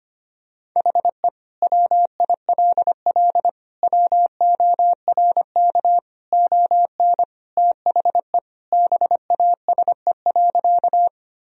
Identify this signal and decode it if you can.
Morse code, heard as a keyed tone, about 25 wpm, 700 Hz. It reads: HE WILL WORK ON THE BASE.
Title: Sentences from Top 1000 Words - Review Part 1 - 25wpm